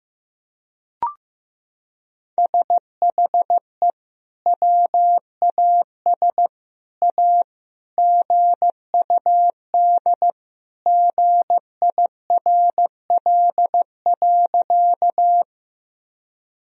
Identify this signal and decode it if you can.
Morse code, heard as a keyed tone, about 15 wpm, 700 Hz. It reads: SHE WAS A GUD GIRL.
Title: Sentences from Top 400 Words and Abbreviations 15wpm